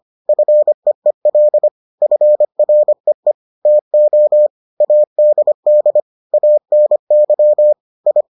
Morse code, heard as a keyed tone, about 25 wpm, 600 Hz. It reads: FEEL FREE TO ADD ANY I